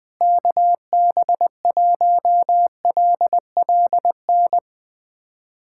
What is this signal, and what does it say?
Morse code, heard as a keyed tone, about 20 wpm, 700 Hz. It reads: KB1LLN
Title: USA Callsign DE Callsign - 6 Characters - 20wpm